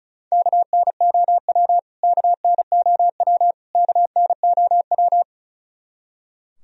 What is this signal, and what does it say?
Morse code, heard as a keyed tone, about 34 wpm, 700 Hz. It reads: KNOW KNOW KNOW
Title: Top 100 Words Repeated 35wpm